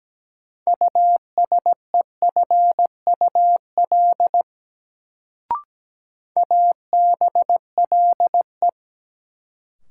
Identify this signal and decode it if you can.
Morse code, heard as a keyed tone, about 17 wpm, 700 Hz. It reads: USEFUL ABLE
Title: Sentences from Top 700 Words - New Word Drill - 17wpm